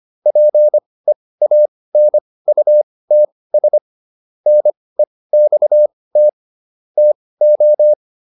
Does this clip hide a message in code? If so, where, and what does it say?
Yes: Morse code from 0.3 to 7.9 s, heard as a keyed tone, about 25 wpm, spaced as for 18 wpm, 600 Hz. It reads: PEANUTS NEXT TO